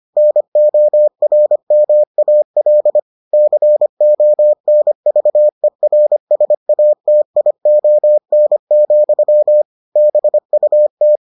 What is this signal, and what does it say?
Morse code, heard as a keyed tone, about 25 wpm, 600 Hz. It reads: NORMAL CONVERSATION, BUT